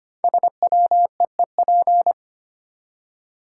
Morse code, heard as a keyed tone, about 25 wpm, 700 Hz. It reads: SWEEP
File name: 5 Letter Words - First 3 Letters Given - 25wpm